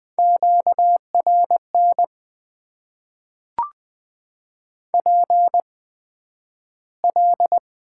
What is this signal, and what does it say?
Morse code, heard as a keyed tone, about 20 wpm, spaced as for 11 wpm, 700 Hz. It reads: QRN PL
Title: Top 90 QSO Elements - 8x Character Spacing - 20wpm